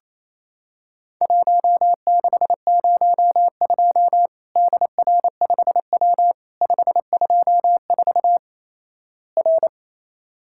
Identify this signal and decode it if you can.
Morse code, heard as a keyed tone, about 28 wpm, 700 Hz. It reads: 1602 DR5W 524 R
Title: Worked All Europe DX Contest - QTC Exchange - 28wpm